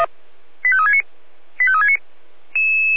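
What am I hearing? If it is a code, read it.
Five-tone selective call (ZVEI), heard as quick runs of five tones, tones of about 70 ms. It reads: 85369 85369
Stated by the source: ZVEI five-tone selective call received on SDR.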